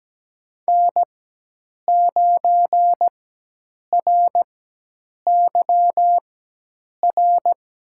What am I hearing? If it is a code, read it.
Morse code, heard as a keyed tone, about 17 wpm, spaced as for 8 wpm, 700 Hz. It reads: N9RYR
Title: USA Call Signs - 4x Character Spacing - 17wpm